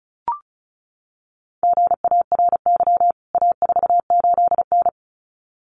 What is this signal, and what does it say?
Morse code, heard as a keyed tone, about 34 wpm, 700 Hz. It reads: GARY A48N
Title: CWT Exchange 35wpm